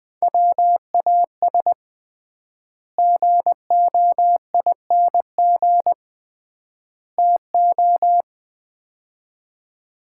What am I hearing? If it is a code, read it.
Morse code, heard as a keyed tone, about 21 wpm, 700 Hz. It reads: WAS GOING TO
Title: 3x Word-Spacing - Sets of 3 Words - 20wpm